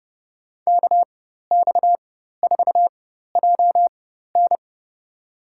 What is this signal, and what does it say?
Morse code, heard as a keyed tone, about 30 wpm, spaced as for 14 wpm, 700 Hz. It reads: KX4JN
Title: USA Call Signs - 5 Characters - 4x Character Spacing - 30wpm